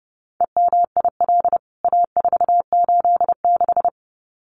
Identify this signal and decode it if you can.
Morse code, heard as a keyed tone, about 30 wpm, 700 Hz. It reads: EMIL A486
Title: CWT Exchange 30wpm